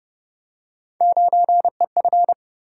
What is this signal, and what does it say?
Morse code, heard as a keyed tone, about 29 wpm, 700 Hz. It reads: 9EF